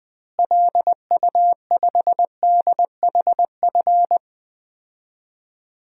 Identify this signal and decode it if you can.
Morse code, heard as a keyed tone, about 20 wpm, 700 Hz. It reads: LU5DHF
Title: DXCC-100 Call Signs - 6 Characters - Repeated 3x - 20wpm